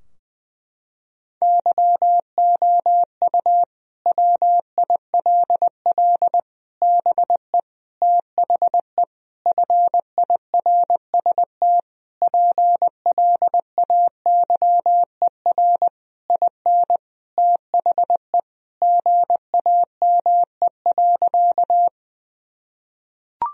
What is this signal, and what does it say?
Morse code, heard as a keyed tone, about 20 wpm, 700 Hz. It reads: YOU WILL BE THE FIRST PLAYER IN THE GAME.
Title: Sentences from Top 1000 Words - Review Part 4 - 20wpm